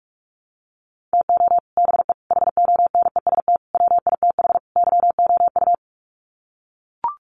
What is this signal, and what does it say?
Morse code, heard as a keyed tone, about 44 wpm, 700 Hz. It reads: TO BE HONEST WITH YOU
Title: Sets of 5 Words Encore 45wpm